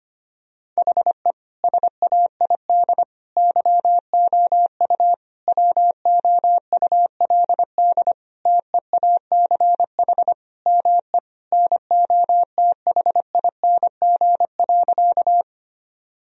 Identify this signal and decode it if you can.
Morse code, heard as a keyed tone, about 25 wpm, 700 Hz. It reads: HE SAID YOU WOULD TEACH ME NOTHING.
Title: Sentences from Top 1000 Words - Review Part 2 - 25wpm